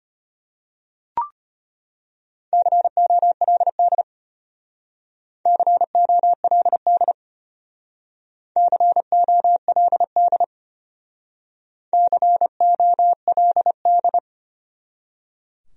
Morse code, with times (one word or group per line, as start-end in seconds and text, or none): 2.5-4.0 COLD
5.5-7.1 COLD
8.6-10.4 COLD
11.9-14.2 COLD